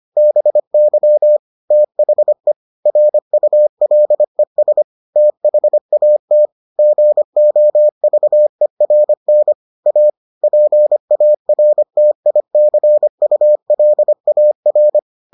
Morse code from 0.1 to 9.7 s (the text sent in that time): BY THE RULES THAT GOVERN